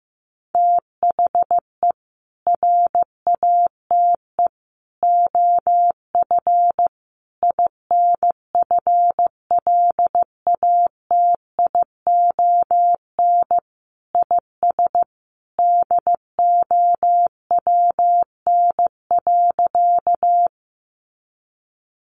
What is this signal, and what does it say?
Morse code, heard as a keyed tone, about 15 wpm, 700 Hz. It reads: THE RATE OF INFLATION IS DOWN.
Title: Sentences from Top 1100 Words 15wpm